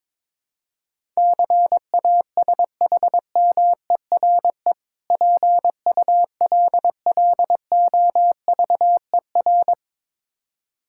Morse code, heard as a keyed tone, about 23 wpm, 690 Hz. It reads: CASHMERE PULLOVER
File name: Difficult Words with Hints - 22wpm